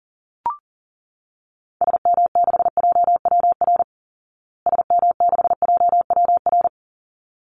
Morse code, heard as a keyed tone, about 40 wpm, 700 Hz. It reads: SM6JWR SM6JWR
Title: Call Signs Repeated 3x - 40wpm